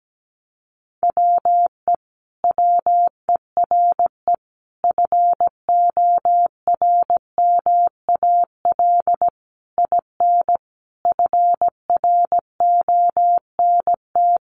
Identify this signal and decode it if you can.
Morse code, heard as a keyed tone, about 17 wpm, 700 Hz. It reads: WE WERE FORMAL IN FRONT
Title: Sentences from Top 1300 Words 17wpm